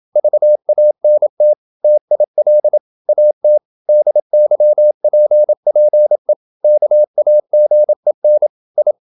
Morse code, heard as a keyed tone, about 27 wpm, 600 Hz. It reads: VANT TIL AT DYPPE KAGEN I
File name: Ferie QRQ 27wpm